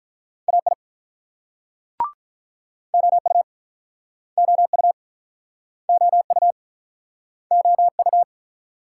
Morse code, heard as a keyed tone, about 48 wpm, 700 Hz. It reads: AI OU OU OU OU